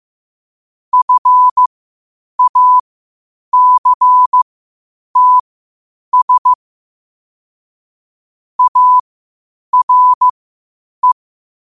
Morse code, heard as a keyed tone, about 15 wpm, spaced as for 8 wpm, 1000 Hz. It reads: FACTS ARE
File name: qotdfiles / 8wpm